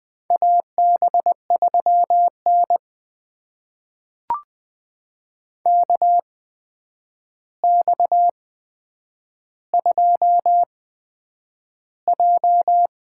Morse code, heard as a keyed tone, about 20 wpm, spaced as for 9 wpm, 700 Hz. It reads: AB3N KX2J
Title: USA Call Signs - 4 Characters - 8x Character Spacing - 20wpm